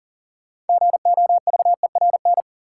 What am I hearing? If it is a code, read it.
Morse code, heard as a keyed tone, about 40 wpm, 700 Hz. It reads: GOVERN
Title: Sentences from Top 1000 Words - Review Part 2 - New Word Drill - 40wpm